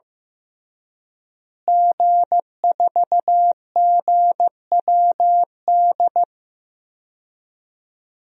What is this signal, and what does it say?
Morse code, heard as a keyed tone, about 15 wpm, 700 Hz. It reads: G4GWD
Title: DXCC-100 Call Signs - 5 Characters - Repeated 2x - 15wpm